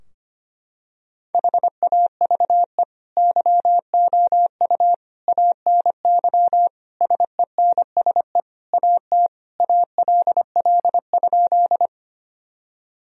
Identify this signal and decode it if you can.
Morse code, heard as a keyed tone, about 25 wpm, 700 Hz. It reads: HAVE YOU ANY SENSE AT ALL?
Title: Sentences from Top 1000 Words - Review Part 4 - 25wpm